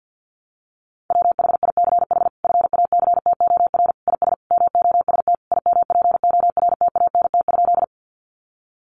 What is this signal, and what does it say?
Morse code, heard as a keyed tone, about 48 wpm, 700 Hz. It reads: WHICH FACTOR IS MOST IMPORTANT?
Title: Sentences from Top 1300 Words 50wpm